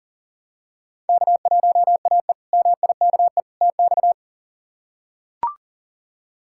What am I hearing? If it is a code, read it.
Morse code, heard as a keyed tone, about 40 wpm, 700 Hz. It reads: K1AE MIKE TX
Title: K1USN SST QSO Practice - 40wpm